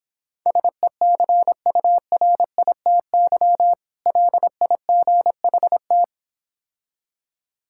Morse code, heard as a keyed tone, about 26 wpm, 700 Hz. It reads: SECURITY LIGHT